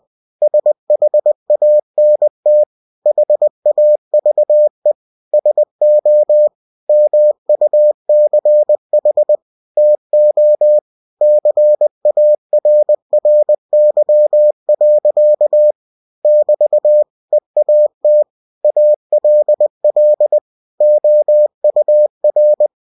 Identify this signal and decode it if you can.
Morse code, heard as a keyed tone, about 20 wpm, 600 Hz. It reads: SHANT HAVE SO MUCH TO CARRY. <BT> EAT ALL OUR
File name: cw-books / winnie-the-pooh / winnie-20wpm-page-0083